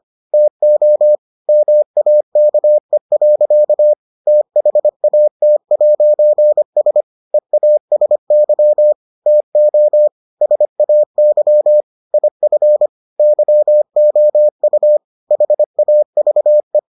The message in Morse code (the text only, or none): TO MAKE. THAT'S EASY TO SAY IF YOU HAVE